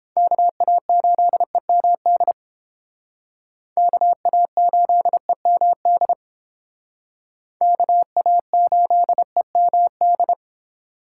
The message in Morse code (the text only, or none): KA8EMD KA8EMD KA8EMD